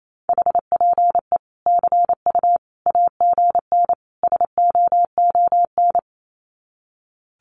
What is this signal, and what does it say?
Morse code, heard as a keyed tone, about 28 wpm, 700 Hz. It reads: HPE CU AGN SOON
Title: Common QSO Phrases 28wpm